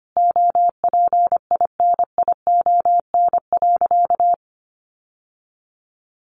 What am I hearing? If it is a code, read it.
Morse code, heard as a keyed tone, about 25 wpm, 700 Hz. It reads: OPINION.
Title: Sentences from Top 1200 Words 25wpm